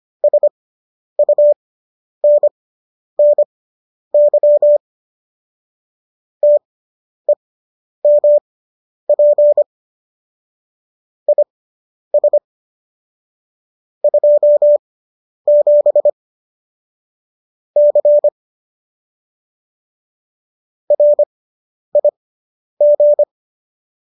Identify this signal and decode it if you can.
Morse code, heard as a keyed tone, about 25 wpm, spaced as for 10 wpm, 590 Hz. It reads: SUNNY TEMP IS 27 C RIG